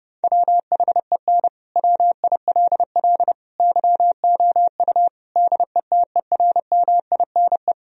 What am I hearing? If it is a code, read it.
Morse code, heard as a keyed tone, about 29 wpm, 700 Hz. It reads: WHEN WILL YOU DETERMINE
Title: Sentences from Top 1000 Words - Review Part 4 - 30wpm